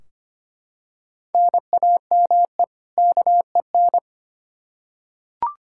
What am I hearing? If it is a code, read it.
Morse code, heard as a keyed tone, about 25 wpm, 700 Hz. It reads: NAME KEN
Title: Practice QSO - Exchange 1 - Beginning - 25wpm